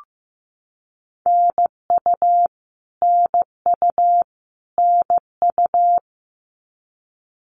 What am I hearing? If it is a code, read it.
Morse code, heard as a keyed tone, about 15 wpm, 700 Hz. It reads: NU NU NU
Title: Canadian Provinces and Territories - Repeated 3x - 15wpm